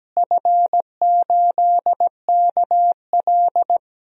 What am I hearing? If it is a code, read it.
Morse code, heard as a keyed tone, about 17 wpm, 700 Hz. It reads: F8KL
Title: DXCC-100 Call Signs - 4 Characters - Repeated 3x - 17wpm